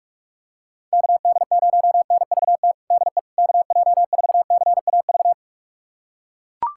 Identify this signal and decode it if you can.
Morse code, heard as a keyed tone, about 41 wpm, 690 Hz. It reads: KD0NVT DE KJ4CAV
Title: USA Callsign DE Callsign 45wpm